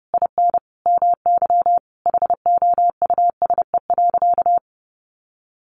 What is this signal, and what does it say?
Morse code, heard as a keyed tone, about 28 wpm, 700 Hz. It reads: IN MY HOUSE.
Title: Sentences from Top 1100 Words 30wpm